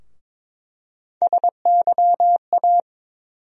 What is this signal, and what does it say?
Morse code, heard as a keyed tone, about 22 wpm, 700 Hz. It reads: SYA
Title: USA Call Sign Postfixes - 3 Letters 22wpm